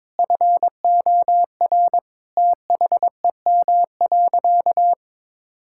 Morse code, heard as a keyed tone, about 22 wpm, 700 Hz. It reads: FOR THEM.
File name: Difficult Sentences with Hints - 22wpm